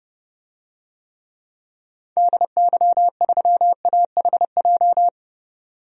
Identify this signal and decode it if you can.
Morse code, heard as a keyed tone, about 30 wpm, 700 Hz. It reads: DY3AHJ
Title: DXCC-100 Call Signs - 6 Characters - 30wpm